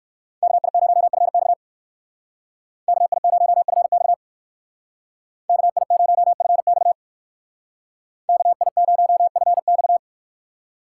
Morse code, heard as a keyed tone, about 61 wpm, 700 Hz. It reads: KI0FX KI0FX KI0FX KI0FX